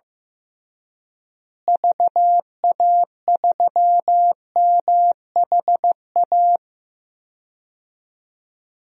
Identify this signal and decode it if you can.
Morse code, heard as a keyed tone, about 15 wpm, 700 Hz. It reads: VA3MHA
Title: DXCC-100 Call Signs - 6 Characters - Repeated 2x - 15wpm